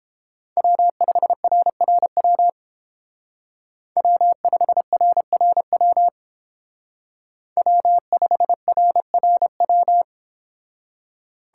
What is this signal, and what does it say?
Morse code, heard as a keyed tone, about 30 wpm, 700 Hz. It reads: W5RRW W5RRW W5RRW